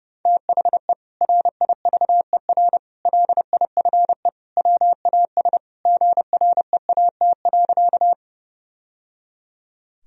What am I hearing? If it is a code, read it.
Morse code, heard as a keyed tone, about 30 wpm, 700 Hz. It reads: THE RIVER LIFE WAS GREAT.
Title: Sentences from Top 1000 Words - Review Part 1 - 30wpm